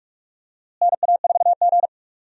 Morse code, heard as a keyed tone, about 44 wpm, 700 Hz. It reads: NA4G
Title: USA Call Signs - 4 Characters - Repeated 3x - 45wpm